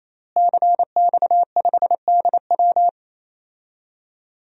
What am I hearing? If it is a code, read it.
Morse code, heard as a keyed tone, about 27 wpm, 700 Hz. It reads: CX5DW
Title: DXCC-100 Call Signs - 5 Characters - Repeated 3x - 28wpm